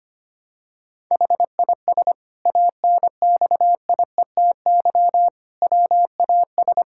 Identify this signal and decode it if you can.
Morse code, heard as a keyed tone, about 25 wpm, 700 Hz. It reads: HIS ANXIETY WAS
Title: Sentences from Top 1600 Words 25wpm